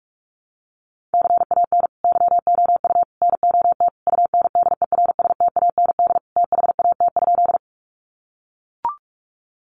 Morse code, heard as a keyed tone, about 44 wpm, 700 Hz. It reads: CAN YOU NOT UNDERSTAND THAT?
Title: Sentences from Top 600 Words 45wpm